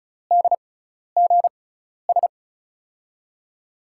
Morse code, heard as a keyed tone, about 36 wpm, spaced as for 12 wpm, 700 Hz. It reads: DGS
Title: USA Call Signs - 5 Characters - 6x Character Spacing - 35wpm